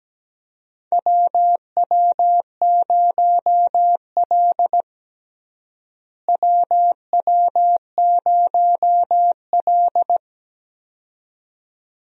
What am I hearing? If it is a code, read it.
Morse code, heard as a keyed tone, about 17 wpm, 700 Hz. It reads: WW0L WW0L
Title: USA Call Signs - 4 Characters - Repeated 2x - 17wpm